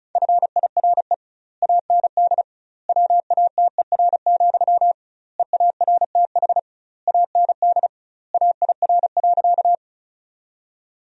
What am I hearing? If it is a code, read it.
Morse code, heard as a keyed tone, about 36 wpm, 700 Hz. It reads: FIRE AND WATER, EARTH AND AIR.
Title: 2x Word-Spacing - Sentences from Top 400 Words - 35wpm